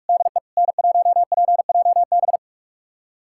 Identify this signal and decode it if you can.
Morse code, heard as a keyed tone, about 45 wpm, 700 Hz. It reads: DE N1PJB
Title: USA Callsign DE Callsign - 5 Characters - 45wpm